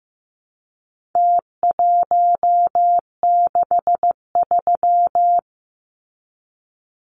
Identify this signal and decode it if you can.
Morse code, heard as a keyed tone, about 15 wpm, 700 Hz. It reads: T163
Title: Numbers - Contest Style - 4-Digit 15wpm